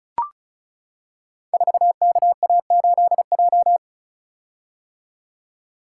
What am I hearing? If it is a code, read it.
Morse code, heard as a keyed tone, about 36 wpm, 690 Hz. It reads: 4KA8J
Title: Code Groups - 5 Letter:Numbers - 35wpm